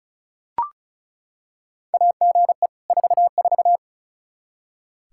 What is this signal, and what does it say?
Morse code, heard as a keyed tone, about 36 wpm, 700 Hz. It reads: AGE 44
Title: Practice QSO - Exchange 3 - More Info - 35wpm